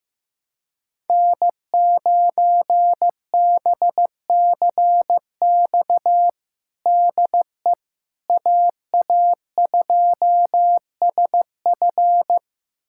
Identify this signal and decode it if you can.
Morse code, heard as a keyed tone, about 15 wpm, 700 Hz. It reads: N9BCX DE AA2SF
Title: USA Callsign DE Callsign - 5 Characters - 15wpm